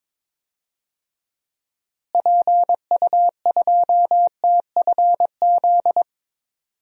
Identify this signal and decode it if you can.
Morse code, heard as a keyed tone, about 22 wpm, 700 Hz. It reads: PU2TFZ
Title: DXCC-100 Call Signs - 6 Characters - Repeated 2x - 22wpm